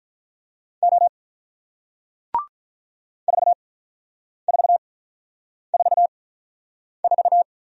Morse code, heard as a keyed tone, about 47 wpm, spaced as for 9 wpm, 700 Hz. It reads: O 4444